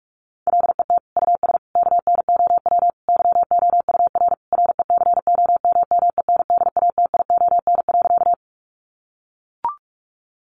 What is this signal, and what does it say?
Morse code, heard as a keyed tone, about 44 wpm, 700 Hz. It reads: LET US KNOW YOUR RECOMMENDATION.
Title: Sentences from Top 1400 Words 45wpm